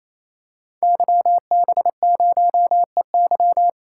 Morse code, heard as a keyed tone, about 29 wpm, 700 Hz. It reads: YB0EY
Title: Code Groups - 5 Letter:Numbers - 28wpm